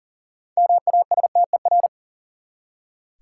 Morse code, heard as a keyed tone, about 40 wpm, 700 Hz. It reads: MASTER
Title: Six Letter Words - 40wpm